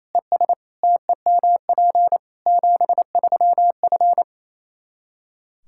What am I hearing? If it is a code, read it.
Morse code, heard as a keyed tone, about 28 wpm, 700 Hz. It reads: ES TEMP 73F
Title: Practice QSO - Exchange 2 - Info - 28wpm